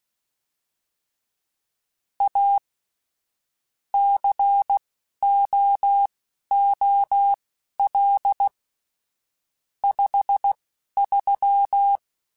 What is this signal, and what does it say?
Morse code, heard as a keyed tone, about 16 wpm, spaced as for 12 wpm, 800 Hz. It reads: A COOL 53
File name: Weather Sets / Weather Sets-mp3s / Weather 5 1 @10